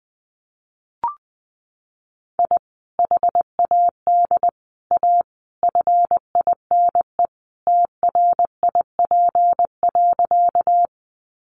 Morse code, heard as a keyed tone, about 20 wpm, 700 Hz. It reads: I HAD A FINE TRIP.
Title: Sentences from Top 1300 Words 20wpm